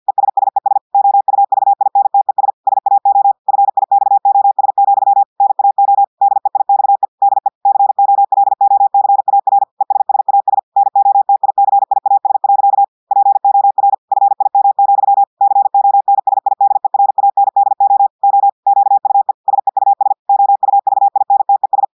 Morse code, heard as a keyed tone, about 50 wpm, 800 Hz. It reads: ESSES OUVINTES SAO FIXOS, NAO DEIXE DE COLOCAR ESSAS NOTICIAS. POR FIM, CONSIDERANDO O QUE SEUS OUVINTES
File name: portugais texte en clair 2 vitesse 50 mots minutes